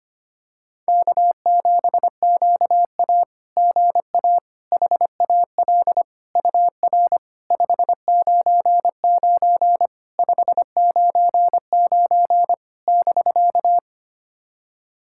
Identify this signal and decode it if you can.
Morse code, heard as a keyed tone, about 25 wpm, 690 Hz. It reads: K7QA GA HAL UR 599 599 <BK>
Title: SOTA QSO Practice 25wpm